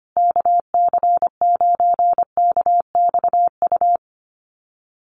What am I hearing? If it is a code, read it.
Morse code, heard as a keyed tone, about 25 wpm, 700 Hz. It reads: KC9KXU
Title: USA Call Signs Repeated 3x 25wpm